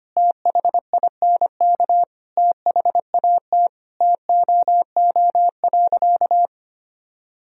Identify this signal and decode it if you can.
Morse code, heard as a keyed tone, about 25 wpm, 700 Hz. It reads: THINK THAT TOO.
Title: Sentences from Top 1000 Words - Review Part 3 - 25wpm